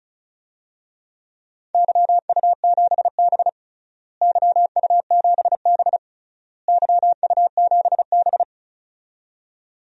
Morse code, heard as a keyed tone, about 34 wpm, 700 Hz. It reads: YU7B YU7B YU7B
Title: DXCC-100 Call Signs - 4 Characters - Repeated 3x - 35wpm